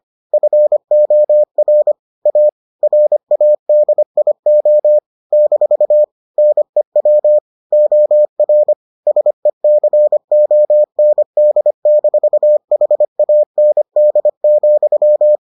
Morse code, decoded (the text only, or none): FOR A RADIO - NEW OR SECOND-HAND,